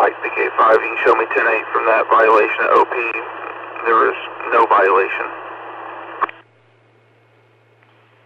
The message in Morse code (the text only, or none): none